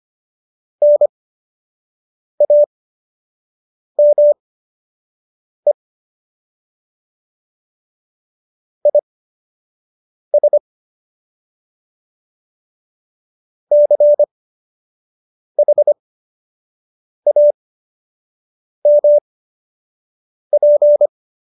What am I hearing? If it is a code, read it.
Morse code, heard as a keyed tone, about 25 wpm, spaced as for 6 wpm, 590 Hz. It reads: NAME IS CHAMP